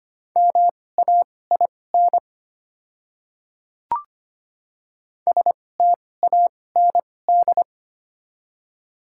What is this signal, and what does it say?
Morse code, heard as a keyed tone, about 25 wpm, spaced as for 18 wpm, 700 Hz. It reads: MAIN STAND
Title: 2x Character Spacing - Top 500 Words - 25wpm